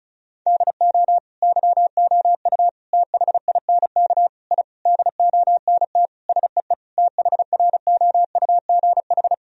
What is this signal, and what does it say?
Morse code, heard as a keyed tone, about 36 wpm, 700 Hz. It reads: DO YOU THINK I DONT SEE THROUGH
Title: Sentences from Top 1000 Words - Review Part 1 - 35wpm